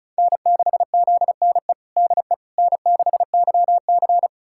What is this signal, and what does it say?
Morse code, heard as a keyed tone, about 36 wpm, 700 Hz. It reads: N6ZNE DE N6YC
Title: USA Callsign DE Callsign 35wpm